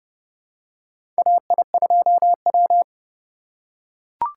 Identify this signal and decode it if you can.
Morse code, heard as a keyed tone, about 30 wpm, 700 Hz. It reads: AI2W
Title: USA Call Signs - 4 Characters - 2x Character Spacing - 30wpm